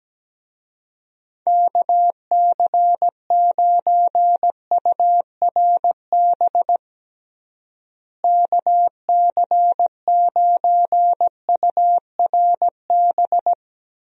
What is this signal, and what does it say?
Morse code, heard as a keyed tone, about 17 wpm, 700 Hz. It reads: KC9URB KC9URB
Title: USA Call Signs - 6 Characters - Repeated 2x - 17wpm